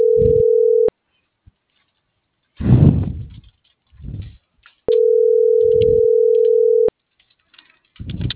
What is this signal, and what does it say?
Ringback tone (North America), 440 + 480 Hz.